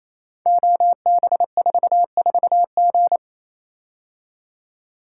Morse code, heard as a keyed tone, about 27 wpm, 700 Hz. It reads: OB44G
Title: Code Groups - 5 Letter:Numbers - 28wpm